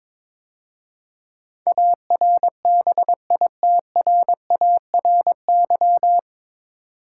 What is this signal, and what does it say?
Morse code, heard as a keyed tone, about 22 wpm, 700 Hz. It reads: ARBITRARY
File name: 9 Letter Words - First 3 Letters Given - 22wpm